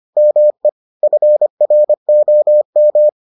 Morse code, heard as a keyed tone, about 25 wpm, 600 Hz. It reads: ME FROM